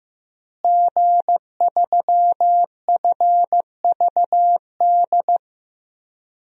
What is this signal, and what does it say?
Morse code, heard as a keyed tone, about 15 wpm, 700 Hz. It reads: G3FVD
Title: DXCC-100 Call Signs - 5 Characters - Repeated 3x - 15wpm